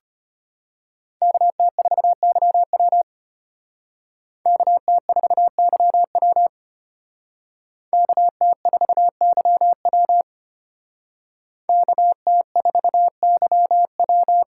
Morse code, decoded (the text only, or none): KT4YW KT4YW KT4YW KT4YW